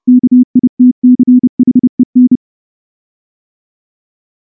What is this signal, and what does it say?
Morse code, heard as a keyed tone, about 29 wpm, 250 Hz. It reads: KITCHEN
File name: morse / morse-training-30WPM